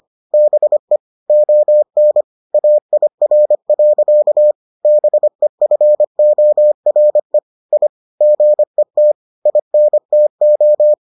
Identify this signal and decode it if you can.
Morse code, heard as a keyed tone, about 25 wpm, 600 Hz. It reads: BE ON AIR. BEFORE I GET INTO